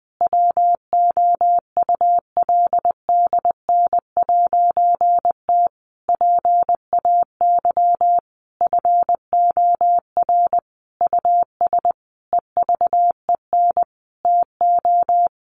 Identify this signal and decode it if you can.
Morse code, heard as a keyed tone, about 20 wpm, 700 Hz. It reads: WOULDN'T PAY FOR US EVEN TO
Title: A Sound Of Thunder 20wpm